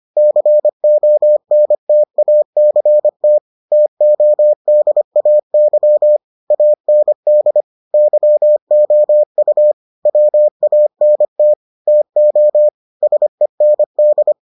Morse code, heard as a keyed tone, about 25 wpm, 600 Hz. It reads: CONTACT TODAY AND YOU WANT TO SEND